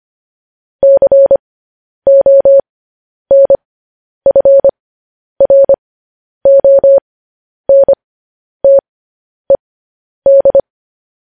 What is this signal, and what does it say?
Morse code, heard as a keyed tone, about 25 wpm, spaced as for 10 wpm, 550 Hz. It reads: CONFRONTED
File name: podcast / News headlines-1x-25x10